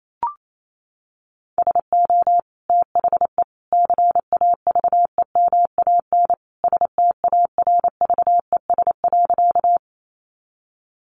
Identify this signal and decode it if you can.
Morse code, heard as a keyed tone, about 28 wpm, 700 Hz. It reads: SO THE CAVEMAN STARVES.
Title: A Sound Of Thunder 28wpm